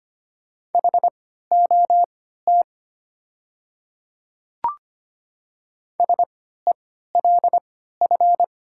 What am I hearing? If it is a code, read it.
Morse code, heard as a keyed tone, about 25 wpm, spaced as for 14 wpm, 700 Hz. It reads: HOT SELF
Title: 3x Character Spacing - Top 200 Words - 25wpm